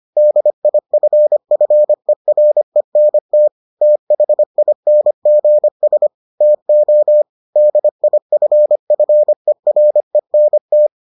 Morse code, heard as a keyed tone, about 25 wpm, 600 Hz. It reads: DIFFERENT THINGS TO DIFFERENT